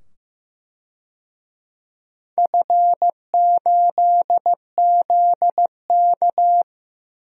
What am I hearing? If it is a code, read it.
Morse code, heard as a keyed tone, about 15 wpm, 700 Hz. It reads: F8ZK
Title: DXCC-100 Call Signs - 4 Characters - 15wpm